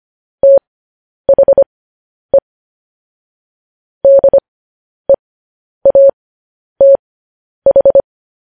Morse code, heard as a keyed tone, about 25 wpm, spaced as for 10 wpm, 550 Hz. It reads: THE DEATH